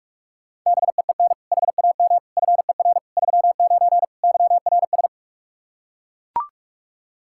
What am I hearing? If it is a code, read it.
Morse code, heard as a keyed tone, about 41 wpm, 700 Hz. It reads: BEEN HAM FER 39 YRS
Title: K1USN SST QSO Practice 45wpm